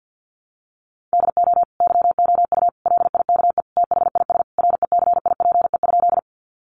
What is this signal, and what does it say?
Morse code, heard as a keyed tone, about 50 wpm, 700 Hz. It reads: DO YOU LIKE THIS RECIPE?
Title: Sentences from Top 1200 Words 50wpm